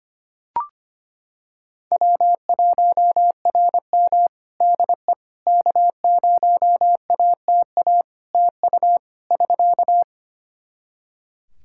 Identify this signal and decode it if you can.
Morse code, heard as a keyed tone, about 25 wpm, 700 Hz. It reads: W1RM DE K0ATA TU <SK>
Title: Practice QSO - Exchange 4 - Ending - 25wpm